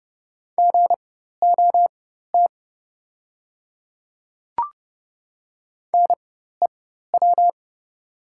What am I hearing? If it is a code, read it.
Morse code, heard as a keyed tone, about 30 wpm, spaced as for 14 wpm, 700 Hz. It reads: GOT NEW
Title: 4x Character Spacing - Top 300 Words - 30wpm